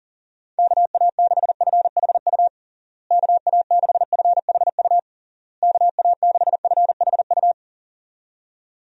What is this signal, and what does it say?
Morse code, heard as a keyed tone, about 40 wpm, 700 Hz. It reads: KA6FHU KA6FHU KA6FHU
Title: USA Call Signs - 6 Characters - Repeated 3x - 40wpm